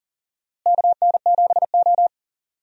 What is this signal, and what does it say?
Morse code, heard as a keyed tone, about 40 wpm, 700 Hz. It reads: KN7O
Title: USA Call Signs - 4 Characters - Repeated 2x - 40wpm